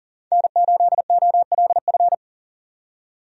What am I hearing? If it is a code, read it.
Morse code, heard as a keyed tone, about 39 wpm, 700 Hz. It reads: N8OLF
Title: USA Call Signs - 5 Characters - 40wpm